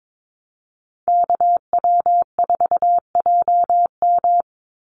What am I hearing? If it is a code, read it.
Morse code, heard as a keyed tone, about 22 wpm, 700 Hz. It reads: KW4JM
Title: NA SOTA Activator Callsigns 22wpm